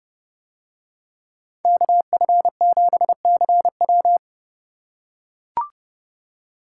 Morse code, heard as a keyed tone, about 28 wpm, 700 Hz. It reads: KF7CW